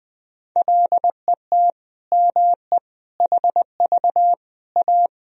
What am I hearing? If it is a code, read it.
Morse code, heard as a keyed tone, about 20 wpm, 700 Hz. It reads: LET ME HV A
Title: Sentences from Top 300 Words and Abbreviations 20wpm